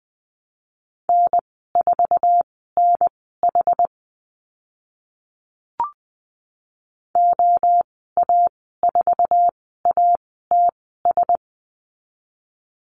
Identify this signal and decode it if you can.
Morse code, heard as a keyed tone, about 20 wpm, spaced as for 14 wpm, 700 Hz. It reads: N4NH OA4ATS
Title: 069 Lesson 12g - Letter d - Call Signs 20wpm